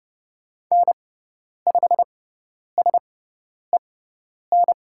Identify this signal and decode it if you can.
Morse code, heard as a keyed tone, about 30 wpm, spaced as for 10 wpm, 700 Hz. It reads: N5SEN
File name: Lesson 18g - Letter m - Call Signs 30wpm